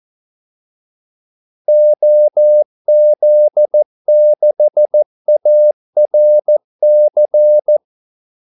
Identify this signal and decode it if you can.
Morse code, heard as a keyed tone, about 14 wpm, 600 Hz. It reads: OZ6ARC
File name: Callsigns-14wpm